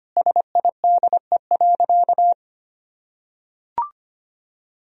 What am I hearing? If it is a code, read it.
Morse code, heard as a keyed tone, about 25 wpm, 700 Hz. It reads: SIDE.
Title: Sentences from Top 1000 Words - Review Part 1 - 25wpm